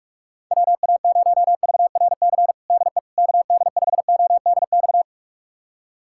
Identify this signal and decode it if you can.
Morse code, heard as a keyed tone, about 44 wpm, 700 Hz. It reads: WA0VRC DE KD5ODX
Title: USA Callsign DE Callsign - 6 Characters - 45wpm